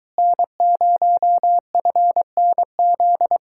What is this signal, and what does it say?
Morse code, heard as a keyed tone, about 23 wpm, 700 Hz. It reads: N0FNZ